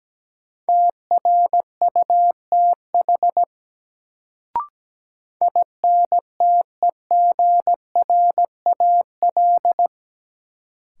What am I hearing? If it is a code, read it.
Morse code, heard as a keyed tone, about 17 wpm, 700 Hz. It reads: TRUTH INTEGRAL
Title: Sets of 5 Words - New Word Drill - 17wpm